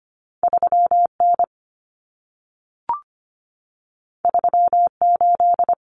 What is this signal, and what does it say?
Morse code, heard as a keyed tone, about 25 wpm, 700 Hz. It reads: 3N 38
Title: Common Cut Numbers - 2-Digit 25wpm